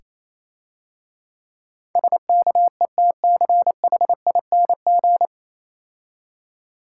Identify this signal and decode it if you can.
Morse code, heard as a keyed tone, about 28 wpm, 690 Hz. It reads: SKETCHING